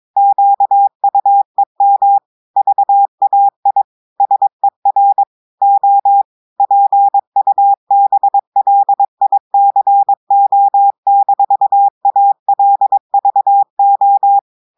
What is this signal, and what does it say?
Morse code, heard as a keyed tone, about 22 wpm, 800 Hz. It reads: QUEM VAI SER O PUBLICO-ALVO
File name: audios / portugais / portugais texte en clair 2 vitesse 22 mots minutes